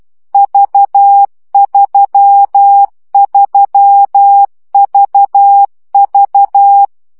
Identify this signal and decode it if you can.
Morse code, heard as a keyed tone, about 12 wpm, 800 Hz. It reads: V33VV
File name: CW gyakorlas v 3 10 baud